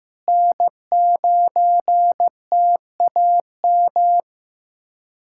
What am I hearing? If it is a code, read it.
Morse code, heard as a keyed tone, about 15 wpm, 700 Hz. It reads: N9TAM
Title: USA Callsign DE Callsign - 5 Characters - 15wpm